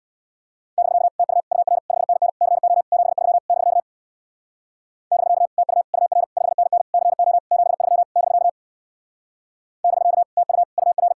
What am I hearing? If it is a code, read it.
Morse code, heard as a keyed tone, about 66 wpm, 700 Hz. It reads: <BK> TU UR 5NN CO 73 <BK> <BK> TU UR 5NN CO 73 <BK> <BK> TU UR